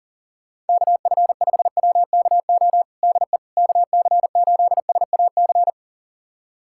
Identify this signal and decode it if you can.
Morse code, heard as a keyed tone, about 40 wpm, 700 Hz. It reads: KF5WKO DE KC8SAC
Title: USA Callsign DE Callsign 40wpm